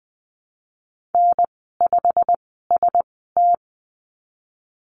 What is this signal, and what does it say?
Morse code, heard as a keyed tone, about 20 wpm, spaced as for 15 wpm, 700 Hz. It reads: N5ST
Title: 128 Lesson 21g - Number 6 - Call Signs 20wpm